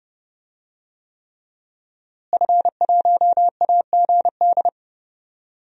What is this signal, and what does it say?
Morse code, heard as a keyed tone, about 30 wpm, 700 Hz. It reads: F1AGD
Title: DXCC-100 Call Signs - 5 Characters - Repeated 3x - 30wpm